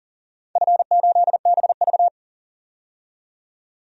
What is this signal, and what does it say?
Morse code, heard as a keyed tone, about 40 wpm, 700 Hz. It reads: F8BV
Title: DXCC-100 Call Signs - 4 Characters - 40wpm